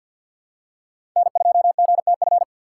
Morse code, heard as a keyed tone, about 52 wpm, 700 Hz. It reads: N2GTF
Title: USA Call Signs - 5 Characters - 2x Character Spacing - 50wpm